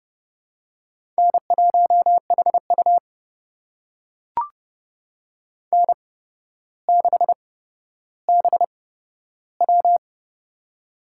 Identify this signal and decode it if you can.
Morse code, heard as a keyed tone, about 30 wpm, spaced as for 13 wpm, 700 Hz. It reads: N1HU N6BW
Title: USA Call Signs - 4 Characters - 8x Character Spacing - 30wpm